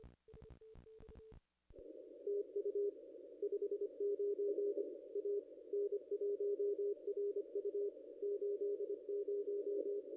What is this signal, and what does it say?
Morse code, heard as a keyed tone, about 26 wpm, 440 Hz. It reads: TU 59 A N1RU 80
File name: contests / ss-cw-k5tr-op-n5rz -0200